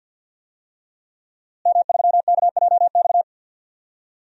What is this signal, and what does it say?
Morse code, heard as a keyed tone, about 52 wpm, 700 Hz. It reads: M3KJX
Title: DXCC-100 Call Signs - 5 Characters - Repeated 3x - 50wpm